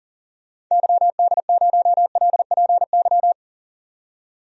Morse code, heard as a keyed tone, about 40 wpm, 700 Hz. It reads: YD0LPY
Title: DXCC-100 Call Signs - 6 Characters - 40wpm